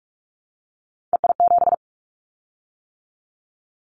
Morse code, heard as a keyed tone, about 46 wpm, 700 Hz. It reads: EI7
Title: Callsign Prefixes for Top 50 DXCC - 45wpm